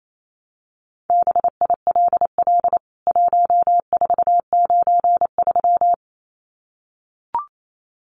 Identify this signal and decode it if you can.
Morse code, heard as a keyed tone, about 28 wpm, 700 Hz. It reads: BILL 1493
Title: CWT Exchange 28wpm